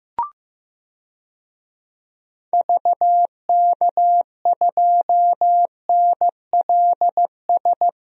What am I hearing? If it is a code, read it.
Morse code, heard as a keyed tone, about 15 wpm, 700 Hz. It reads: VK2NLS
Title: DXCC-100 Call Signs - 6 Characters - Repeated 3x - 15wpm